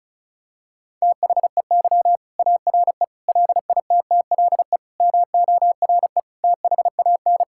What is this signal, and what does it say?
Morse code, heard as a keyed tone, about 36 wpm, 700 Hz. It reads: THEY ARE LITTLE MORE THAN
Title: Sentences from Top 1600 Words 35wpm